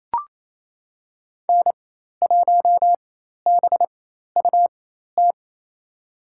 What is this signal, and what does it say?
Morse code, heard as a keyed tone, about 28 wpm, spaced as for 13 wpm, 700 Hz. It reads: N1BUT
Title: USA Call Signs - 4x Character Spacing - 28wpm